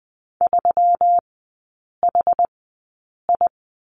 Morse code, heard as a keyed tone, about 20 wpm, spaced as for 8 wpm, 700 Hz. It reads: 3HI
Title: 121 Lesson 20e - Number 3 - Three Characters 20wpm